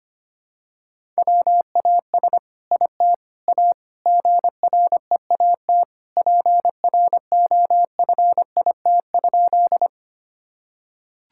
Morse code, heard as a keyed tone, about 25 wpm, 700 Hz. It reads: WAS IT A GREAT PROFIT?
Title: Sentences from Top 1000 Words - Review Part 5 - 25wpm